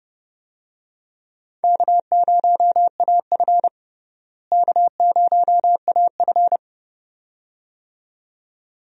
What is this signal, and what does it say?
Morse code, heard as a keyed tone, about 30 wpm, 700 Hz. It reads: K0AF K0AF
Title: USA Call Signs - 4 Characters - Repeated 2x - 30wpm